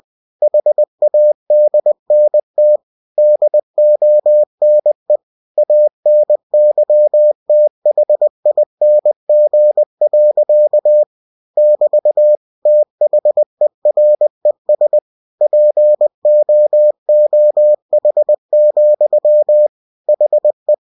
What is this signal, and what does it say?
Morse code, heard as a keyed tone, about 20 wpm, 600 Hz. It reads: HADNT DONE ANYTHING. <BT> THERES POOH, HE
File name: cw-books / winnie-the-pooh / winnie-20wpm-page-0091